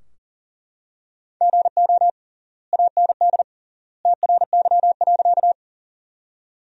Morse code, heard as a keyed tone, about 40 wpm, 700 Hz. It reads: GO AND TRY.
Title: 3x Word-Spacing - Sentences from Top 200 Words - 40wpm